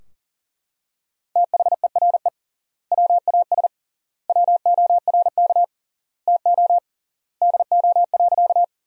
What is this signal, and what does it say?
Morse code, heard as a keyed tone, about 40 wpm, 700 Hz. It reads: THERE WAS WORK TO DO.
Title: 3x Word-Spacing - Sentences from Top 200 Words - 40wpm